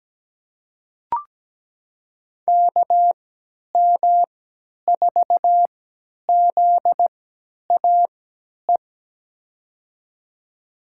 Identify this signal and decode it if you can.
Morse code, heard as a keyed tone, about 17 wpm, spaced as for 10 wpm, 700 Hz. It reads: KM4ZAE
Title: USA Call Signs - 3x Character Spacing - 17wpm